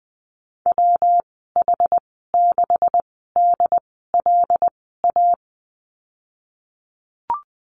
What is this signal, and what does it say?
Morse code, heard as a keyed tone, about 20 wpm, spaced as for 15 wpm, 700 Hz. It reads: WH6DLA
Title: 167 Lesson 27g - Number 7 - Call Signs 20wpm